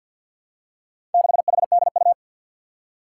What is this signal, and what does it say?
Morse code, heard as a keyed tone, about 50 wpm, 700 Hz. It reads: 6HDU
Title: Four Challenging Letter-Number Combinations 50wpm